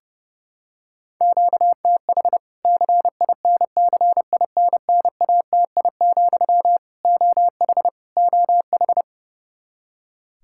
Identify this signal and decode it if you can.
Morse code, heard as a keyed tone, about 32 wpm, 700 Hz. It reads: QTH CINCINNATI, OH OH
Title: K1USN SST QSO Practice 30wpm